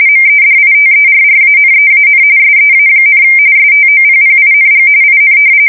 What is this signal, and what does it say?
RTTY, about 45.45 baud. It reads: MEDITERRANEAN SEA, STOPPING IN
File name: fsk170 / battleshipnj